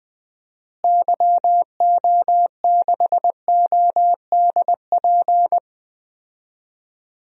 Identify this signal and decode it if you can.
Morse code, heard as a keyed tone, about 20 wpm, 700 Hz. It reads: YO6ODP
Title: DXCC-100 Call Signs - 6 Characters - Repeated 2x - 20wpm